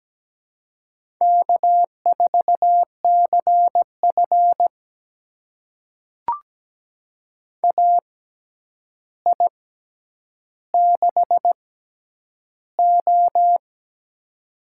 Morse code, heard as a keyed tone, about 17 wpm, spaced as for 9 wpm, 700 Hz. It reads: K4CF AI6O
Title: USA Call Signs - 6x Character Spacing - 17wpm